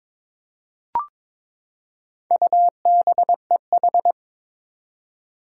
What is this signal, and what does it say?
Morse code, heard as a keyed tone, about 22 wpm, 700 Hz. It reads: UBEH
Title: Four Challenging Letter-Number Combinations 22wpm